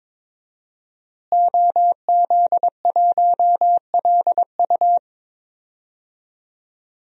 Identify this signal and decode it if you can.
Morse code, heard as a keyed tone, about 22 wpm, 700 Hz. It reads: OZ1LU
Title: DXCC-100 Call Signs - 5 Characters - Repeated 2x - 22wpm